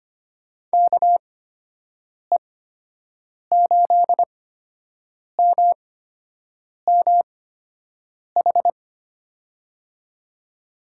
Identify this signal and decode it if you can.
Morse code, heard as a keyed tone, about 25 wpm, spaced as for 7 wpm, 700 Hz. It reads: KE8MMH
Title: USA Call Signs - 6 Characters - 8x Character Spacing - 25wpm